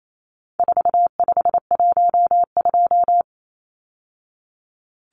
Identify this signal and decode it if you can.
Morse code, heard as a keyed tone, about 28 wpm, 700 Hz. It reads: 4512
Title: Numbers - 4-Digit 28wpm